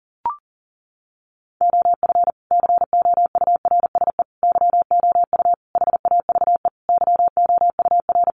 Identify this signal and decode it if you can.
Morse code, heard as a keyed tone, about 40 wpm, 700 Hz. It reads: OF COURSE YOU HAVE YOUR
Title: Sentences from Top 1200 Words 40wpm